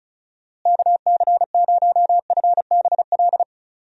Morse code, heard as a keyed tone, about 34 wpm, 700 Hz. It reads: KC0FBL
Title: USA Call Signs - 6 Characters - Repeated 2x - 35wpm